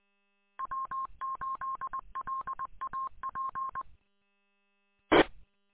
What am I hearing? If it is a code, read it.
Morse code, heard as a keyed tone, about 21 wpm, 1060 Hz. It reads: W8LAP T
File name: w8lap- -combined